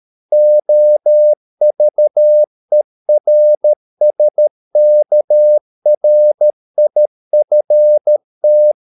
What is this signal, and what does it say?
Morse code, heard as a keyed tone, about 13 wpm, 600 Hz. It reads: OVERSKRIFT
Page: Ferie 13 wpm